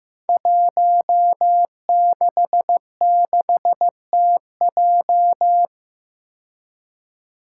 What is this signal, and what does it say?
Morse code, heard as a keyed tone, about 15 wpm, 700 Hz. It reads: 166TJ
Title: Five Challenging Letter-Number Combinations 15wpm